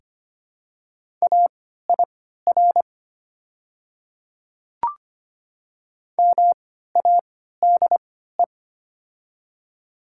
Morse code, heard as a keyed tone, about 25 wpm, spaced as for 14 wpm, 700 Hz. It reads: AIR MADE
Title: 3x Character Spacing - Top 300 Words - 25wpm